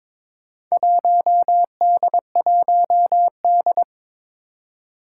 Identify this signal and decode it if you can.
Morse code, heard as a keyed tone, about 22 wpm, 700 Hz. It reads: 1D1D
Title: Four Challenging Letter-Number Combinations 22wpm